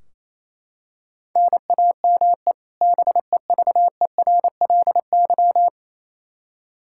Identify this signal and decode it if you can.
Morse code, heard as a keyed tone, about 28 wpm, 700 Hz. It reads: NAME BEVERLY
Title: Practice QSO - Exchange 1 - Beginning - 28wpm